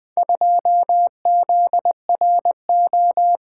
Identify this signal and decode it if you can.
Morse code, heard as a keyed tone, about 20 wpm, 700 Hz. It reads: 2ZRO
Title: Four Letter-Number Combinations - 20wpm